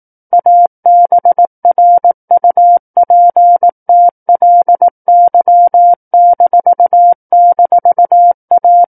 Morse code, heard as a keyed tone, about 18 wpm, 700 Hz. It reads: ABRUPTLY--A
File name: 18 WPM